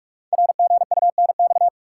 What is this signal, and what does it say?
Morse code, heard as a keyed tone, about 44 wpm, 700 Hz. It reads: RGUNX
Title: Code Groups - 5 Letters - 45wpm